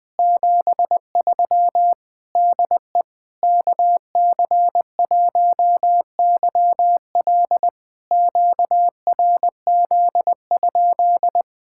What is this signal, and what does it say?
Morse code, heard as a keyed tone, about 20 wpm, 700 Hz. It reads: 73 DE KC1YL QRZ?
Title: POTA QSO Practice - 20wpm